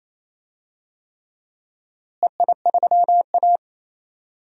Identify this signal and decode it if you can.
Morse code, heard as a keyed tone, about 28 wpm, 700 Hz. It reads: EI3A
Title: DXCC-100 Call Signs - 4 Characters - Repeated 3x - 28wpm